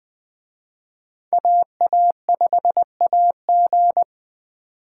Morse code, heard as a keyed tone, about 20 wpm, 700 Hz. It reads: AA5AG
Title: Speed Racing - USA Call Signs 17wpm